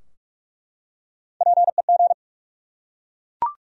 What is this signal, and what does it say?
Morse code, heard as a keyed tone, about 43 wpm, 700 Hz. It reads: PEG
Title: USA Common Names - Max 3 Letters - 45wpm